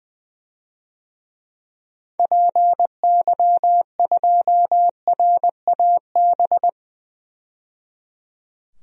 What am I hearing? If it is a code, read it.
Morse code, heard as a keyed tone, about 20 wpm, 700 Hz. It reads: PY2RAB
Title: DXCC-100 Call Signs - 6 Characters - 20wpm